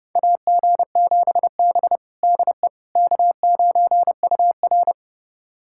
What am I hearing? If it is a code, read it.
Morse code, heard as a keyed tone, about 31 wpm, 700 Hz. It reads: AG7B DE K9UR
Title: USA Callsign DE Callsign - 4 Characters - 30wpm